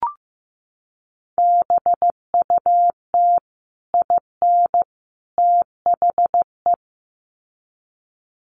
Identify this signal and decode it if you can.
Morse code, heard as a keyed tone, about 15 wpm, 700 Hz. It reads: BUT IN THE
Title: Sets of 3 Words 15wpm